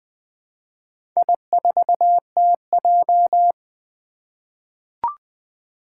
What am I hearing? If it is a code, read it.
Morse code, heard as a keyed tone, about 20 wpm, 700 Hz. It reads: I4TJ
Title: Four Challenging Letter-Number Combinations 20wpm